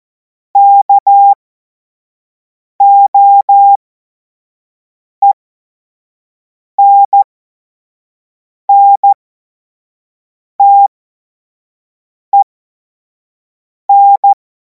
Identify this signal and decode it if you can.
Morse code, heard as a keyed tone, about 14 wpm, spaced as for 5 wpm, 800 Hz. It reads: KOENNTEN
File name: Morsekurs / Witze / Witz 10. März 2026 25 F70